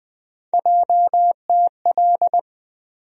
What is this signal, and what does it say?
Morse code, heard as a keyed tone, about 21 wpm, 700 Hz. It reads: JTL
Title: USA Call Sign Postfixes - 3 Letters 20wpm